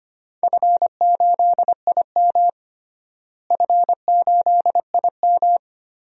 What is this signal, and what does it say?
Morse code, heard as a keyed tone, about 25 wpm, 700 Hz. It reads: F8IM F8IM
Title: DXCC-100 Call Signs - 4 Characters - Repeated 3x - 25wpm